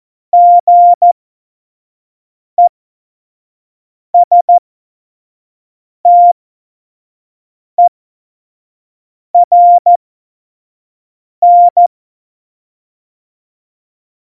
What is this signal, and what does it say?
Morse code, heard as a keyed tone, about 14 wpm, spaced as for 5 wpm, 700 Hz. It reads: GESTERN
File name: Morsekurs / Witze / Witz 12. September 17, Tempo 25